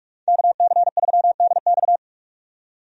Morse code, heard as a keyed tone, about 45 wpm, 700 Hz. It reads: KC3DX
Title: USA Call Signs - 5 Characters - 45wpm